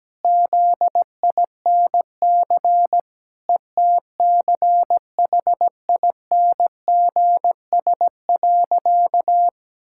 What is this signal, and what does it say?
Morse code, heard as a keyed tone, about 17 wpm, 700 Hz. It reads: ZINC ETCHINGS.
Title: Pangram Challenge v2 - 17wpm